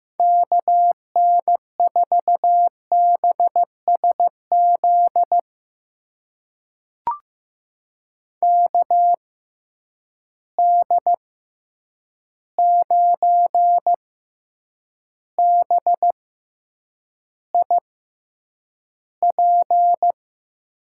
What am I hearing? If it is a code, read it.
Morse code, heard as a keyed tone, about 15 wpm, spaced as for 8 wpm, 700 Hz. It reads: KN4BSZ KD9BIP
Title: USA Call Signs - 6 Characters - 6x Character Spacing - 15wpm